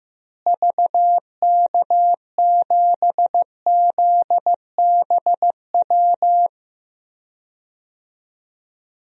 Morse code, heard as a keyed tone, about 15 wpm, 700 Hz. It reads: VK7ZBW
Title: DXCC-100 Call Signs - 6 Characters - Repeated 2x - 15wpm